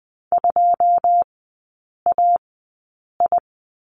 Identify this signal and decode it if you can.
Morse code, heard as a keyed tone, about 20 wpm, spaced as for 8 wpm, 700 Hz. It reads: 2AI
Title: 100 Lesson 17e - Letter u - Three Characters 20wpm